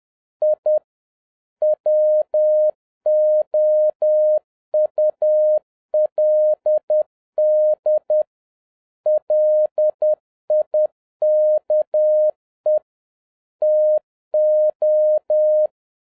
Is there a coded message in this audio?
Yes: Morse code from 0.4 to 15.7 s, heard as a keyed tone, about 10 wpm, 610 Hz. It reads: I WOULD LIKE TO